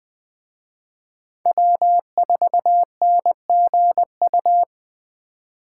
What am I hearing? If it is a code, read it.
Morse code, heard as a keyed tone, about 20 wpm, 700 Hz. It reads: W4NGU
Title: Speed Racing - USA 1x3 Call Signs 17wpm